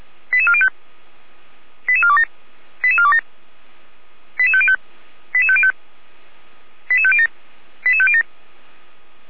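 Five-tone selective call (ZVEI), heard as quick runs of five tones, tones of about 70 ms. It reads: 88405 88427 88427 88505 88505 88507 88507
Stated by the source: ZVEI five-tone selective call 88447 received by SDR.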